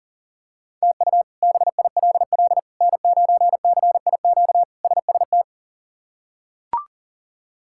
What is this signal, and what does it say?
Morse code, heard as a keyed tone, about 40 wpm, 700 Hz. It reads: TU BILL N9CIQ SST
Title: K1USN SST QSO Practice - 40wpm